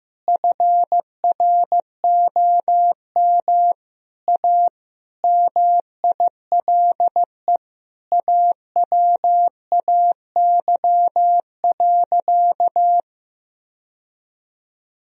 Morse code, heard as a keyed tone, about 16 wpm, 700 Hz. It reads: FROM A MILE AWAY.
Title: Sentences from Top 1000 Words - Review Part 3 - 15wpm